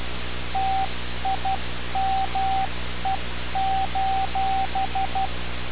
Morse code, heard as a keyed tone, about 12 wpm, 770 Hz, with radio noise. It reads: TIME:
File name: radiocap / REC z